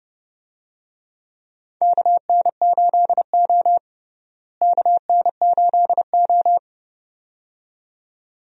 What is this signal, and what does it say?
Morse code, heard as a keyed tone, about 30 wpm, 700 Hz. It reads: KN8O KN8O
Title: USA Call Signs - 4 Characters - Repeated 2x - 30wpm